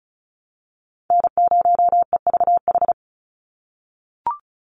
Morse code, heard as a keyed tone, about 34 wpm, 700 Hz. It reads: N0EVH
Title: NA SOTA Chaser Callsigns 35wpm